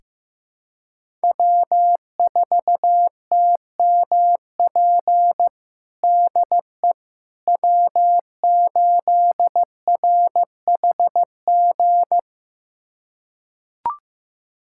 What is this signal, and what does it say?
Morse code, heard as a keyed tone, about 15 wpm, 700 Hz. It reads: W4TMP DE W8RHG
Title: USA Callsign DE Callsign - 5 Characters - 15wpm